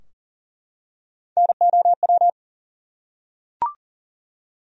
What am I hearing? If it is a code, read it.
Morse code, heard as a keyed tone, about 40 wpm, 700 Hz. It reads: NOW
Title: Top 100 Words Repeated 40wpm